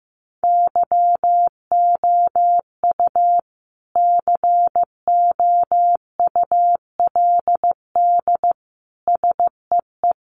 Morse code, heard as a keyed tone, about 15 wpm, 700 Hz. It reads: YOU COULD SEE
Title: Sentences from Top 500 Words 15wpm